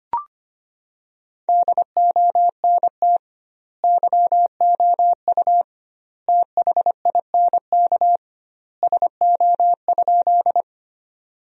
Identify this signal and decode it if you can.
Morse code, heard as a keyed tone, about 25 wpm, 700 Hz. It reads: DONT YOU THINK SO?
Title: 2x Word-Spacing - Sentences from Top 300 Words - 25wpm